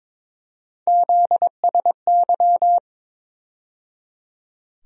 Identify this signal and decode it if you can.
Morse code, heard as a keyed tone, about 22 wpm, 700 Hz. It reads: ZSY
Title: USA Call Sign Postfixes - 3 Letters 22wpm